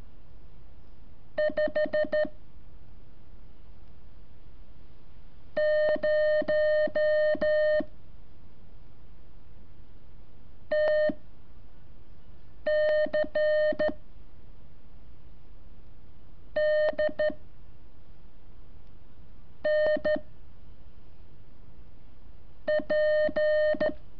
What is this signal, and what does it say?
Morse code, heard as a keyed tone, about 13 wpm, spaced as for 5 wpm, 610 Hz. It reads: 5 0 TC D N P
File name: curso / ejercicio 23